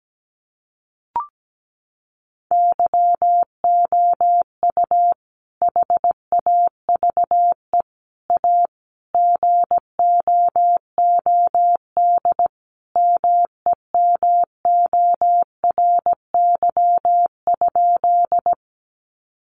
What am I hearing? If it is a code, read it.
Morse code, heard as a keyed tone, about 17 wpm, 700 Hz. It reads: YOU HAVE A GOOD MEMORY?
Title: Sentences from Top 1300 Words 17wpm